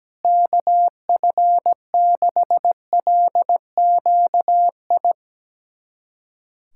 Morse code, heard as a keyed tone, about 17 wpm, 700 Hz. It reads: KF6LQI
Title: USA Callsign DE Callsign - 6 Characters - 17wpm